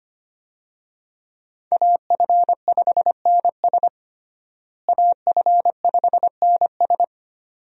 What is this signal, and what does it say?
Morse code, heard as a keyed tone, about 25 wpm, 700 Hz. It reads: AF5NS AF5NS
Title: USA Call Signs - 5 Characters - Repeated 2x - 25wpm